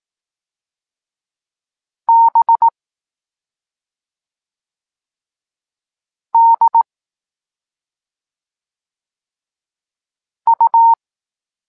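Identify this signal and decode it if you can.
Morse code, heard as a keyed tone, about 18 wpm, 920 Hz. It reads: B D U
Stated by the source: Evenly spaced Morse lesson text 21 for learning letters G and Z, from SP9MOA Niepołomicki Klub Krótkofalowców.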